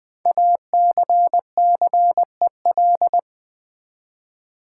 Morse code, heard as a keyed tone, about 20 wpm, 700 Hz. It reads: ACCEL
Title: Build A Word - 5 Letters At A Time - 10-Letter Words - 20wpm